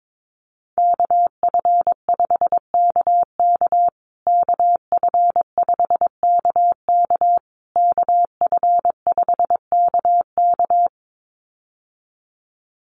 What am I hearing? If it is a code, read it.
Morse code, heard as a keyed tone, about 22 wpm, 700 Hz. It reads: KF5KK KF5KK KF5KK
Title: USA Call Signs Repeated 3x 22wpm